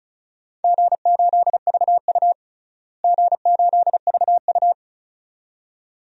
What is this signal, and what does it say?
Morse code, heard as a keyed tone, about 34 wpm, 700 Hz. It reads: G8VU G8VU
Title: DXCC-100 Call Signs - 4 Characters - Repeated 2x - 35wpm